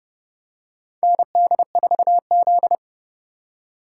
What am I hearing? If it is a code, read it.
Morse code, heard as a keyed tone, about 30 wpm, 700 Hz. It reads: ND4Z
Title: USA Call Signs - 4x Character Spacing - 30wpm